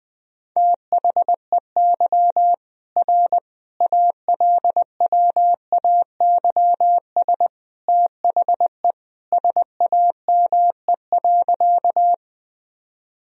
Morse code, heard as a keyed tone, about 20 wpm, 700 Hz. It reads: THEY R ALWAYS THE SAME.
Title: Sentences from Top 300 Words and Abbreviations 20wpm